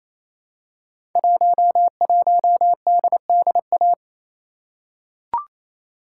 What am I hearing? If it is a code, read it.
Morse code, heard as a keyed tone, about 27 wpm, 690 Hz. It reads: 11DDA
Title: Code Groups - 5 Letter:Numbers - 28wpm